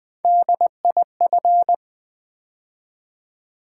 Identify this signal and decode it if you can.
Morse code, heard as a keyed tone, about 20 wpm, 700 Hz. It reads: DIF
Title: Build A Word - 3 Letters At A Time - 9-Letter Words - 20wpm